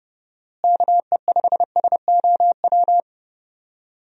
Morse code, heard as a keyed tone, about 30 wpm, 700 Hz. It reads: KE5SOW
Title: USA Call Signs - 6 Characters - Repeated 3x - 30wpm